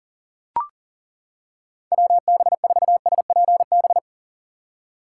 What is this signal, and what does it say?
Morse code, heard as a keyed tone, about 40 wpm, 700 Hz. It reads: WB4SPB
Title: K1USN SST QSO Practice - 40wpm